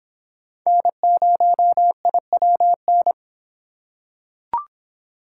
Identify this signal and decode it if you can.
Morse code, heard as a keyed tone, about 26 wpm, 700 Hz. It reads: N0IWN